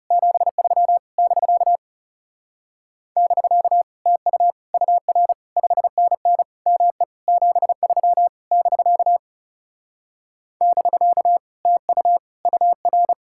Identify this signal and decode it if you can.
Morse code, heard as a keyed tone, about 37 wpm, 700 Hz. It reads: 73 <BK> <BK> TU UR 5NN ME 73 <BK> <BK> TU UR
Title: Speed-Racing Common QSO Phrases Encore - 30wpm